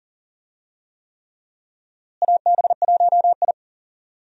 Morse code, heard as a keyed tone, about 40 wpm, 700 Hz. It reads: AB1I
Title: USA Call Signs - 4 Characters - Repeated 3x - 40wpm